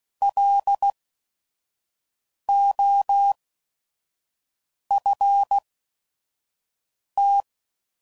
Morse code, heard as a keyed tone, about 16 wpm, spaced as for 5 wpm, 800 Hz. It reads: LOFT
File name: Prefixes / By First Letter / Ls